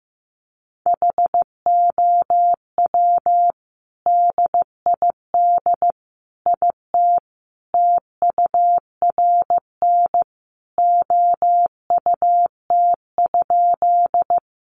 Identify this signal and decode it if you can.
Morse code, heard as a keyed tone, about 15 wpm, 700 Hz. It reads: HOW DID IT TURN OUT?
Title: A Sound Of Thunder 15wpm